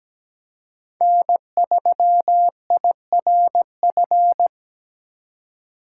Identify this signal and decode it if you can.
Morse code, heard as a keyed tone, about 17 wpm, 700 Hz. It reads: N3IRF
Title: USA Call Signs - 4x Character Spacing - 17wpm